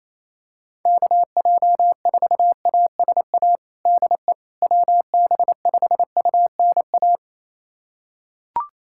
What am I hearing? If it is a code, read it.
Morse code, heard as a keyed tone, about 28 wpm, 700 Hz. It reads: KJ4ASA DE WB5UNA
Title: USA Callsign DE Callsign - 6 Characters - 28wpm